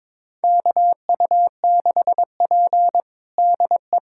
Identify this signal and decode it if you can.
Morse code, heard as a keyed tone, about 22 wpm, 700 Hz. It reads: KU6P DE
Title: USA Callsign DE Callsign - 4 Characters - 22wpm